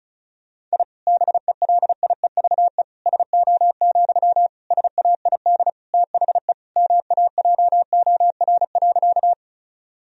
Morse code, heard as a keyed tone, about 36 wpm, 700 Hz. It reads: I BELIEVE SO, SAID THE MAJOR.
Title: Sentences from Top 1000 Words - Review Part 4 - 35wpm